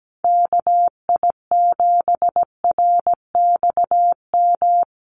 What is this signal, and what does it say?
Morse code, heard as a keyed tone, about 17 wpm, 700 Hz. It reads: KI7RXM
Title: USA Call Signs 17wpm